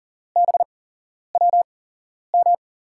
Morse code, heard as a keyed tone, about 40 wpm, 700 Hz. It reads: B W M
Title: USA Call Signs - 6 Characters - 8x Character Spacing - 40wpm